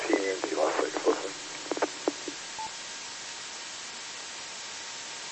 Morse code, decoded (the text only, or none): none